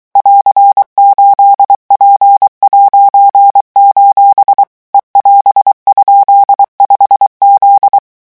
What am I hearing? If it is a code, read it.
Morse code, heard as a keyed tone, about 23 wpm, 800 Hz. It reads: <AR>8P': E<AS>?5Z
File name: morse mp3 / revision19-vitesse 22 mots